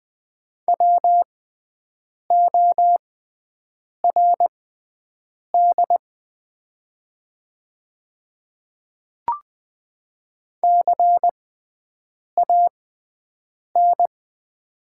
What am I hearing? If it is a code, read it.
Morse code, heard as a keyed tone, about 20 wpm, spaced as for 7 wpm, 700 Hz. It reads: WORD CAN
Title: 6x Character Spacing - Top 300 Words - 20wpm